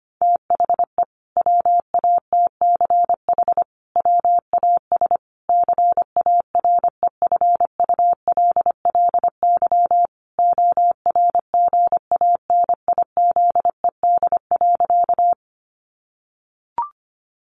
Morse code, heard as a keyed tone, about 25 wpm, 700 Hz. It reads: THE WATCH WAS CAREFULLY ORGANIZED.
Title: Sentences from Top 1300 Words 25wpm